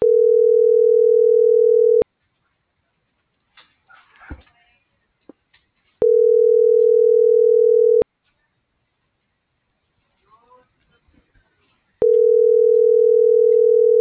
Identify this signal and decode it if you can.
Ringback tone (North America), 440 + 480 Hz.